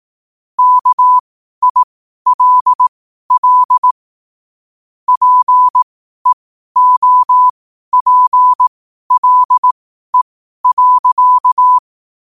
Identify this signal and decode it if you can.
Morse code, heard as a keyed tone, about 18 wpm, spaced as for 12 wpm, 1000 Hz. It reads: KILL PEOPLE.
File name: 12wpm